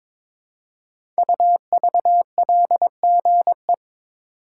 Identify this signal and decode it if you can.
Morse code, heard as a keyed tone, about 22 wpm, 700 Hz. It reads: UVLGE
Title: Code Groups - 5 Letter:Numbers - 22wpm